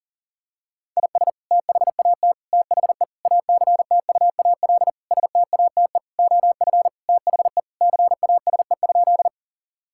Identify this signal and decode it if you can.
Morse code, heard as a keyed tone, about 40 wpm, 700 Hz. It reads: IS THAT THE ACTUAL STATE OF THE CASE?
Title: Sentences from Top 1000 Words - Review Part 5 - 40wpm